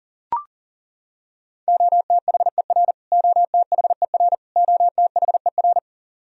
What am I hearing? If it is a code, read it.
Morse code, heard as a keyed tone, about 40 wpm, 700 Hz. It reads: OTHER OTHER OTHER
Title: Top 100 Words Repeated 40wpm